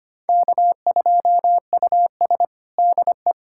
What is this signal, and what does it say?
Morse code, heard as a keyed tone, about 25 wpm, 700 Hz. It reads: K2US DE
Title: USA Callsign DE Callsign - 4 Characters - 25wpm